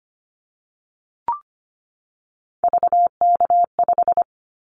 Morse code, heard as a keyed tone, about 25 wpm, 700 Hz. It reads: VK5
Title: Callsign Prefixes for Top 50 DXCC - 25wpm